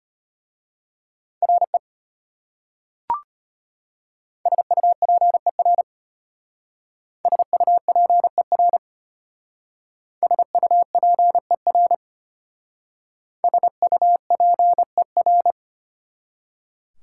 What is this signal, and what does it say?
Morse code, heard as a keyed tone, about 32 wpm, 700 Hz. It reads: RE SUPER SUPER SUPER SUPER